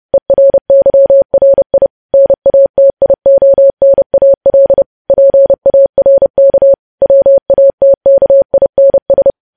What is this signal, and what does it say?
Morse code, heard as a keyed tone, about 30 wpm, 550 Hz. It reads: ERYRI NATIONAL PARK WATKINS
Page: news headlines in morse code at 30 wpm